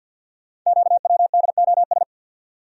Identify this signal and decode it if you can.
Morse code, heard as a keyed tone, about 47 wpm, 700 Hz. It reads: QWDOS
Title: Code Groups - 5 Letters - 50wpm